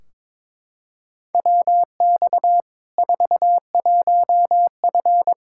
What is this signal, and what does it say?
Morse code, heard as a keyed tone, about 22 wpm, 700 Hz. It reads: WX 41F
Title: Practice QSO - Exchange 2 - Info - 22wpm